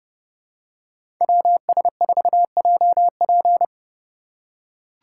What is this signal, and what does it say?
Morse code, heard as a keyed tone, about 30 wpm, 700 Hz. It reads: WS4JP
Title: USA Call Signs - 5 Characters - 2x Character Spacing - 30wpm